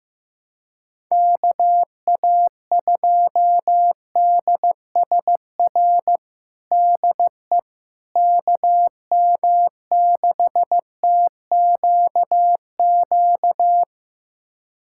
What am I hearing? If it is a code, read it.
Morse code, heard as a keyed tone, about 15 wpm, 700 Hz. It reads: KA2DSR DE KM6TQQ
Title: USA Callsign DE Callsign 15wpm